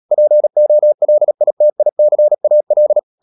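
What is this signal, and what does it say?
Morse code, heard as a keyed tone, about 37 wpm, 600 Hz. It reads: POLITICAL
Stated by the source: Morse code practice at 35–39 wpm sending English text of Ken Follett's novel Never, from CW med Gnister.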